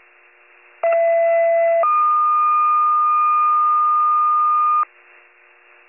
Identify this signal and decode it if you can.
Two-tone page, heard as a one-second tone, then a three-second tone, about 670 then 1220 Hz. It is heard from 0.8 to 4.8 s.